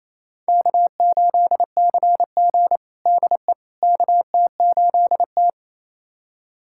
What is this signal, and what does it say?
Morse code, heard as a keyed tone, about 27 wpm, 700 Hz. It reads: K8CG DE KT8T
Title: USA Callsign DE Callsign - 4 Characters - 28wpm